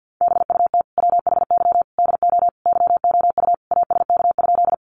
Morse code, heard as a keyed tone, about 50 wpm, 700 Hz. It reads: BUT WHY DO YOU ASK?
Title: Sentences from Top 200 Words 50wpm